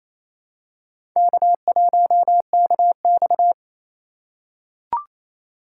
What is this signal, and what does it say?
Morse code, heard as a keyed tone, about 28 wpm, 690 Hz. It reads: K1KX
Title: USA Call Signs - 4 Characters - 2x Character Spacing - 28wpm